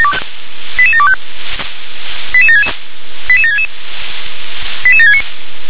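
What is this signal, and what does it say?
Five-tone selective call (ZVEI), heard as quick runs of five tones, tones of about 70 ms. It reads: 88725 88766 88766 88769
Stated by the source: ZVEI five-tone selective call received on SDR.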